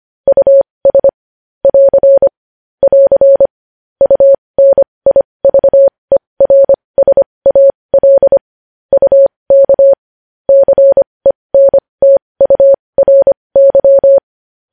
Morse code, heard as a keyed tone, about 25 wpm, spaced as for 20 wpm, 550 Hz. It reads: US <AR> <AR> UNIVERSAL UK CENTURY
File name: podcast / News headlines-1x-25x20